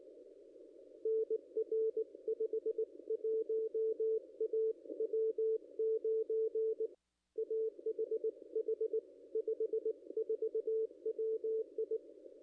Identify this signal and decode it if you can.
Morse code, heard as a keyed tone, about 19 wpm, 440 Hz, with radio noise. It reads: NR 51 A W9AHH 54 WI
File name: contests / ss-cw-k5tr-op-n5rz -0100